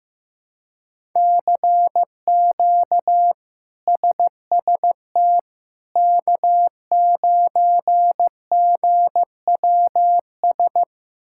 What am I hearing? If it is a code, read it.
Morse code, heard as a keyed tone, about 16 wpm, 700 Hz. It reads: CQ SST K9GWS
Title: K1USN SST QSO Practice - 15wpm